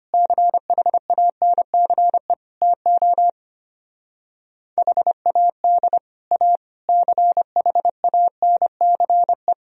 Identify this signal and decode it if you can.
Morse code, heard as a keyed tone, about 32 wpm, 700 Hz. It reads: CHANCE TO HAD A CHANCE